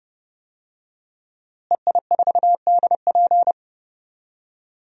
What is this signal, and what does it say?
Morse code, heard as a keyed tone, about 30 wpm, 700 Hz. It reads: EI4DP
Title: DXCC-100 Call Signs - 5 Characters - Repeated 2x - 30wpm